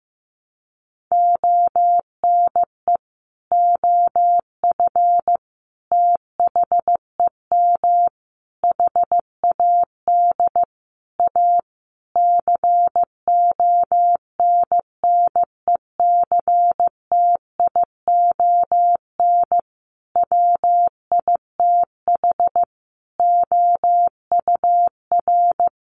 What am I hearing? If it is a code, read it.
Morse code, heard as a keyed tone, about 15 wpm, 700 Hz. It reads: ONE OF THEM HAD A CONNECTION WITH OUR
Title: Sentences from Top 1300 Words 15wpm